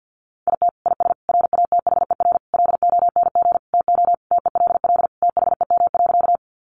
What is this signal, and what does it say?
Morse code, heard as a keyed tone, about 50 wpm, 700 Hz. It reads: IT IS RATHER LONG TO TELL THEM.
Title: Sentences from Top 700 Words 50wpm